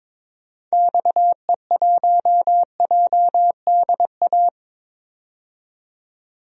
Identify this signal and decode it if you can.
Morse code, heard as a keyed tone, about 22 wpm, 700 Hz. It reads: XE1JDA
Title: DXCC-100 Call Signs - 6 Characters - 22wpm